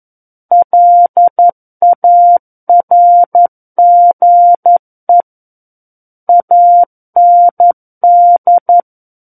Morse code, heard as a keyed tone, about 11 wpm, 700 Hz. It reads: LARGE AND
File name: Book One / 02-Chapter 2 - 11 WPM